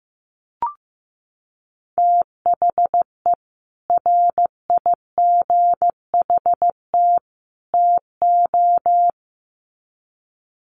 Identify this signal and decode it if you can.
Morse code, heard as a keyed tone, about 15 wpm, 700 Hz. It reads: THE RIGHT TO
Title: Sets of 3 Words 15wpm